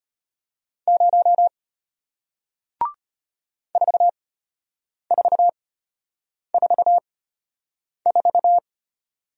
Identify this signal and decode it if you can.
Morse code, heard as a keyed tone, about 34 wpm, spaced as for 8 wpm, 700 Hz. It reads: 0 4444